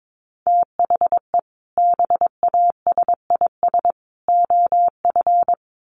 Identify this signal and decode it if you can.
Morse code, heard as a keyed tone, about 22 wpm, 700 Hz. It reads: THE BASIS OF
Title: Sets of 3 Words 22wpm